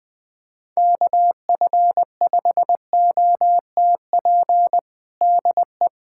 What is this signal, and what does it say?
Morse code, heard as a keyed tone, about 21 wpm, 700 Hz. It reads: KF5OTP DE
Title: USA Callsign DE Callsign 20wpm